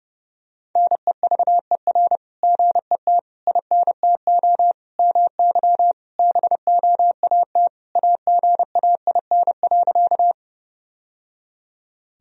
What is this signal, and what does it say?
Morse code, heard as a keyed tone, about 30 wpm, 700 Hz. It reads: NEVER GET INTO MY BOAT AGAIN.
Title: Sentences from Top 1000 Words - Review Part 3 - 30wpm